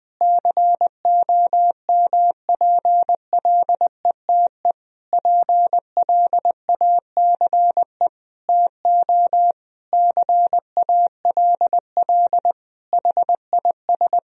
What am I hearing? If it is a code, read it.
Morse code, heard as a keyed tone, about 20 wpm, 700 Hz. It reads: COMPLETE PLACE TO CALL HIS
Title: Sentences from Top 1000 Words - Review Part 2 - 20wpm